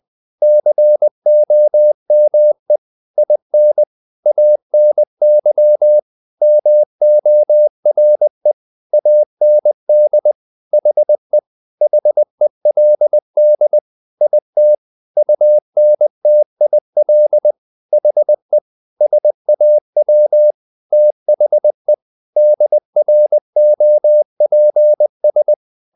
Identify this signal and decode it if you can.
Morse code, heard as a keyed tone, about 20 wpm, 600 Hz. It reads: COME IN ANY MORE AND HE HELD IT UNTIL HE SAW THE DROPS
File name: the old man and the sea part 2 20wpm